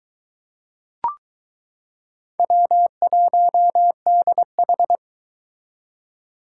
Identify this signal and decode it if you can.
Morse code, heard as a keyed tone, about 22 wpm, 700 Hz. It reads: W1DH